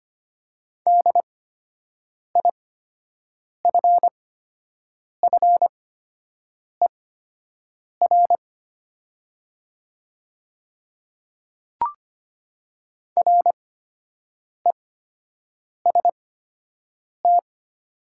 Morse code, heard as a keyed tone, about 25 wpm, spaced as for 7 wpm, 690 Hz. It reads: DIFFER REST
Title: 8x Character Spacing - Top 500 Words - 25wpm